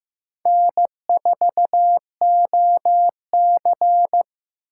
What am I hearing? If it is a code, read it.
Morse code, heard as a keyed tone, about 18 wpm, 700 Hz. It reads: N4OC
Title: Speed Racing - USA 1x2 Call Signs 15wpm